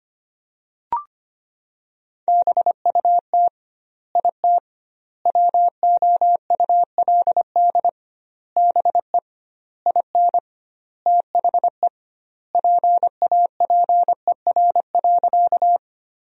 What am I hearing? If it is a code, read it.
Morse code, heard as a keyed tone, about 25 wpm, 700 Hz. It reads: BUT IT WOULD BE IN THE PAPER.
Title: 2x Word-Spacing - Sentences from Top 300 Words - 25wpm